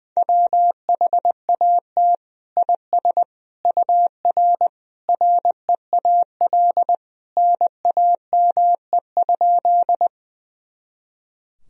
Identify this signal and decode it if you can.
Morse code, heard as a keyed tone, about 20 wpm, 700 Hz. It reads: WHAT IS UR REAL NAME?
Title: Sentences from Top 300 Words and Abbreviations 20wpm